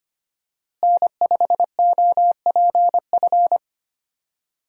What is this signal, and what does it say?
Morse code, heard as a keyed tone, about 25 wpm, 700 Hz. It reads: N5OPF
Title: USA Call Signs - 5 Characters - Repeated 2x - 25wpm